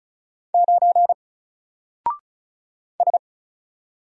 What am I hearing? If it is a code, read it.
Morse code, heard as a keyed tone, about 33 wpm, 700 Hz. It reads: 9 S
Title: Single Letter-Number 35wpm